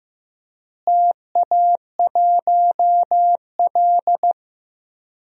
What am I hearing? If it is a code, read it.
Morse code, heard as a keyed tone, about 15 wpm, 700 Hz. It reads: TA1L
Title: DXCC-100 Call Signs - 4 Characters - Repeated 2x - 15wpm